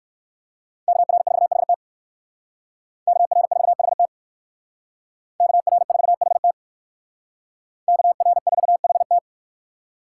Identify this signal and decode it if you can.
Morse code, heard as a keyed tone, about 57 wpm, 700 Hz. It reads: KR4HT KR4HT KR4HT KR4HT